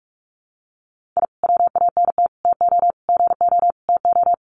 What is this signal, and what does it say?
Morse code, heard as a keyed tone, about 44 wpm, 700 Hz. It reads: I WANT TO GO TO
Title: Sets of 5 Words Encore 45wpm